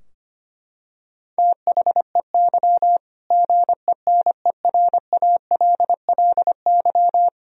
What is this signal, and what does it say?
Morse code, heard as a keyed tone, about 25 wpm, 700 Hz. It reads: THEY GENERALLY
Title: Sentences from Top 1000 Words - Review Part 4 - 25wpm